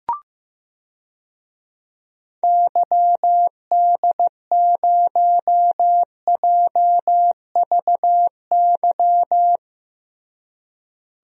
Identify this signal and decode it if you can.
Morse code, heard as a keyed tone, about 15 wpm, 700 Hz. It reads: YD0JVY
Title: DXCC-100 Call Signs - 6 Characters - Repeated 2x - 15wpm